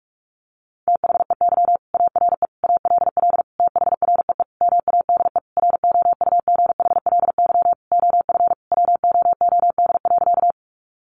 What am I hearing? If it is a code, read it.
Morse code, heard as a keyed tone, about 44 wpm, 700 Hz. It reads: THEY ARE ALL THREE MADE ROUGHLY OF WOOD.
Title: Sentences from Top 1500 Words 45wpm